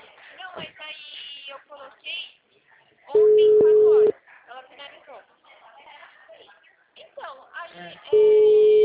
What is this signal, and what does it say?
Ringback tone (Europe, 425 Hz), with speech.
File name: out- -2017- - - .55559